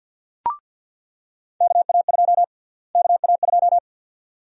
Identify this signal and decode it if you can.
Morse code, heard as a keyed tone, about 49 wpm, 700 Hz. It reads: KA2 KA2
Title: USA Call Sign Prefixes - Repeated 2x - 50wpm